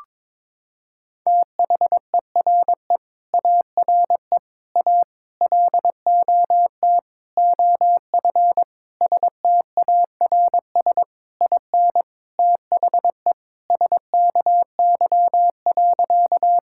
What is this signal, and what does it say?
Morse code, heard as a keyed tone, about 22 wpm, 700 Hz. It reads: THERE ARE A LOT OF STARS IN THE SKY.
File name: practice / 22wpm / Sentences with 4-Word Phrases - 22wpm